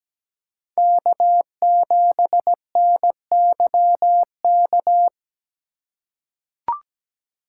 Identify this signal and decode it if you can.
Morse code, heard as a keyed tone, about 17 wpm, 700 Hz. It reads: K7NYK
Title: USA Call Signs - 5 Characters - 3x Character Spacing - 17wpm